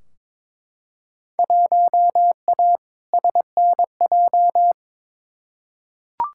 Morse code, heard as a keyed tone, about 22 wpm, 700 Hz. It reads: 1A SNJ
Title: Field Day Exchanges - 22wpm